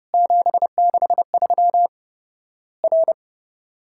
Morse code, heard as a keyed tone, about 30 wpm, 700 Hz. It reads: 763 R
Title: Worked All Europe DX Contest - QTC Exchange - 30wpm